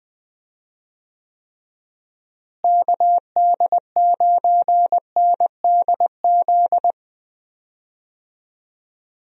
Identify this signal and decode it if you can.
Morse code, heard as a keyed tone, about 21 wpm, 700 Hz. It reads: KD9NDZ
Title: USA Call Signs - 6 Characters - Repeated 3x - 20wpm